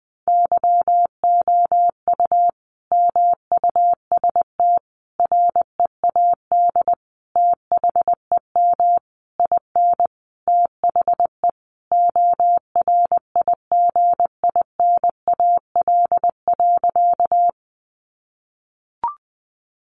Morse code, heard as a keyed tone, about 20 wpm, 700 Hz. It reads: YOU MUST READ THEM IN THE ORIGINAL.
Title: Sentences from Top 900 Words 20wpm